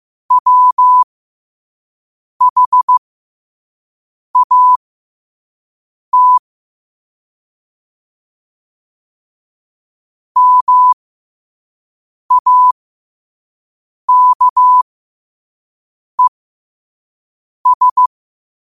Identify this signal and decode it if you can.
Morse code, heard as a keyed tone, about 15 wpm, spaced as for 5 wpm, 1000 Hz. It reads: WHAT MAKES
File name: qotdfiles / 5wpm